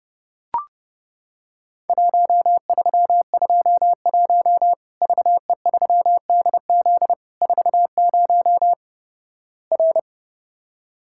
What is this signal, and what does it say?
Morse code, heard as a keyed tone, about 29 wpm, 700 Hz. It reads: 1321 VE3DZ 40 R
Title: Worked All Europe DX Contest - QTC Exchange - 30wpm